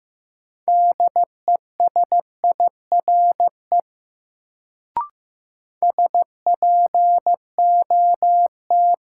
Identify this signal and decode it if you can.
Morse code, heard as a keyed tone, about 15 wpm, 700 Hz. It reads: DESIRE SPOT
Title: Sentences from Top 1100 Words - New Word Drill Alt - 15wpm